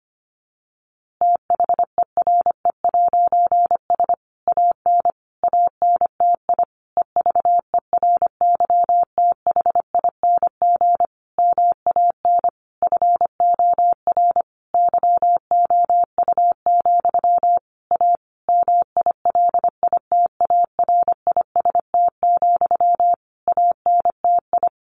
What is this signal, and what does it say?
Morse code, heard as a keyed tone, about 25 wpm, 700 Hz. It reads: THERE'S AN ANTI EVERYTHING MAN FOR YOU, A MILITARIST, ANTI
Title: A Sound Of Thunder 25wpm